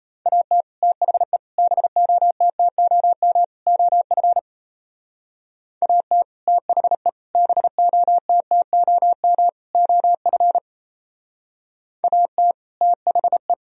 Morse code, heard as a keyed tone, about 32 wpm, 700 Hz. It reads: AT THE BOTTOM OF AT THE BOTTOM OF AT THE
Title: Speed-Racing - Sets of 4 Words 28wpm